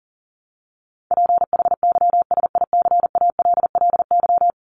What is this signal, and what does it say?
Morse code, heard as a keyed tone, about 40 wpm, 700 Hz. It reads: PHYSICALLY
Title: Top 1501-1750 Words 40wpm